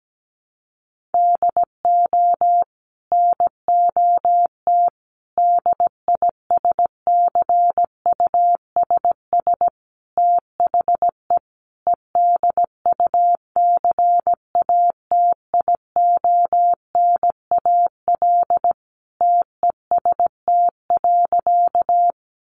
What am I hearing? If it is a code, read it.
Morse code, heard as a keyed tone, about 17 wpm, 700 Hz. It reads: DO NOT DISCUSS THE EDUCATIONAL TEST.
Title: Sentences from Top 1200 Words 17wpm